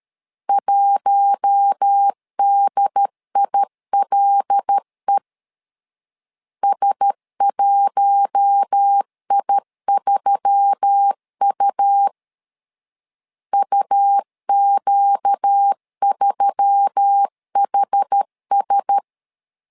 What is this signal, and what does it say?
Morse code, heard as a keyed tone, about 13 wpm, 800 Hz. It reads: 1DILE S1I3U UQ3HS
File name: LEZIONI / 5 LBVC / 5 L 12wpm 1913